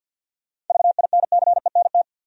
Morse code, heard as a keyed tone, about 52 wpm, 700 Hz. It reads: VINCENT
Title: USA Common Names - 50wpm